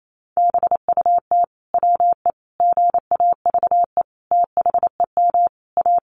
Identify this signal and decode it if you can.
Morse code, heard as a keyed tone, about 28 wpm, 700 Hz. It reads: BUT WE GAVE THEM A
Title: Sentences from Top 1200 Words 28wpm